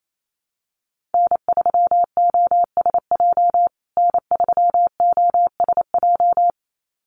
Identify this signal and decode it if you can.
Morse code, heard as a keyed tone, about 28 wpm, 700 Hz. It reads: N3OSJ N3OSJ
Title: USA Call Signs Repeated 2x 28wpm